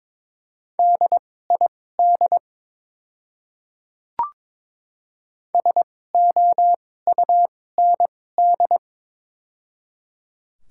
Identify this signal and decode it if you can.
Morse code, heard as a keyed tone, about 22 wpm, spaced as for 16 wpm, 700 Hz. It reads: DID SOUND
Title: 2x Character Spacing - Top 200 Words - 22wpm